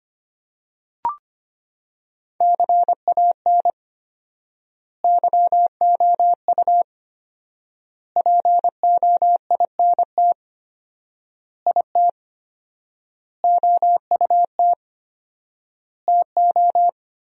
Morse code, heard as a keyed tone, about 25 wpm, 700 Hz. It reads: CAN YOU POINT IT OUT TO
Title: 4x Word-Spacing - Sentences from Top 200 Words - 25wpm